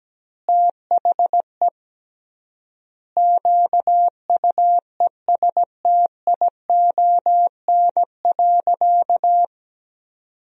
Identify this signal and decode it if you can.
Morse code, heard as a keyed tone, about 17 wpm, 700 Hz. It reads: THE QUESTION.
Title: 3x Word-Spacing - Sentences from Top 400 Words - 17wpm